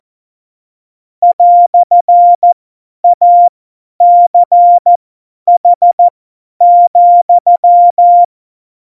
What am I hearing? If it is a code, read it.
Morse code, heard as a keyed tone, about 14 wpm, spaced as for 10 wpm, 700 Hz. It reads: "ACH,
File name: Morsekurs / Witze / Witz 06. November 2018, 50 F70